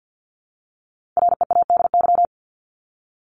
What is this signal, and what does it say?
Morse code, heard as a keyed tone, about 50 wpm, 700 Hz. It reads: READY
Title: Top 500 Words 50wpm